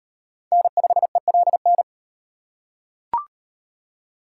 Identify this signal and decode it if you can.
Morse code, heard as a keyed tone, about 37 wpm, 700 Hz. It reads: N5ELN